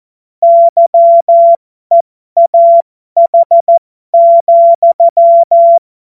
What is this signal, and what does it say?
Morse code, heard as a keyed tone, about 14 wpm, spaced as for 12 wpm, 680 Hz. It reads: YEAH,
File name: Morsekurs / Witze